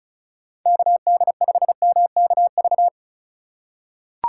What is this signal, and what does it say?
Morse code, heard as a keyed tone, about 36 wpm, 700 Hz. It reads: KD5MKV
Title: USA Call Signs - 2x Character Spacing - 35wpm